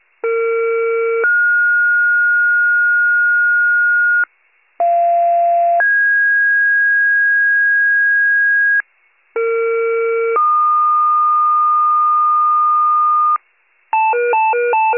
Two-tone page, heard as a one-second tone, then a three-second tone, about 460 then 1430 Hz, about 690 then 1650 Hz, about 460 then 1220 Hz, with noise. It is heard from 0.2 to 13.4 s.